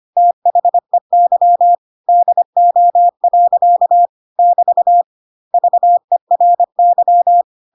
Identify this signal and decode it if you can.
Morse code, heard as a keyed tone, about 25 wpm, 700 Hz. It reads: THEY DO. <BT> VERY
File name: Pinocchio / Pinocchio ch4 25 25 ik1hge